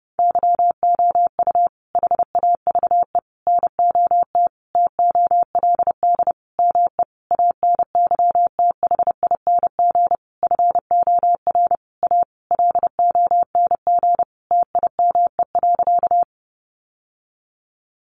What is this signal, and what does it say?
Morse code, heard as a keyed tone, about 28 wpm, 700 Hz. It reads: YOU HAVE NOT TOLD ME ANYTHING FOR A LONG TIME.
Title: Sentences from Top 1000 Words 30wpm